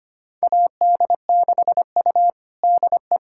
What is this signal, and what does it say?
Morse code, heard as a keyed tone, about 25 wpm, 700 Hz. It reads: AD6U DE
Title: USA Callsign DE Callsign - 4 Characters - 25wpm